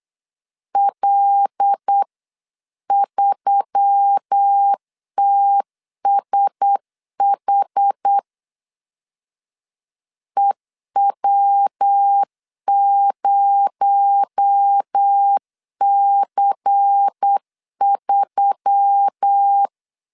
Morse code, heard as a keyed tone, about 8 wpm, 800 Hz. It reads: L3TSH EW0C3
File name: 8wpm